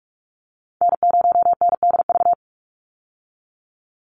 Morse code, heard as a keyed tone, about 44 wpm, 700 Hz. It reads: N0NDV
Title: USA Call Signs 45wpm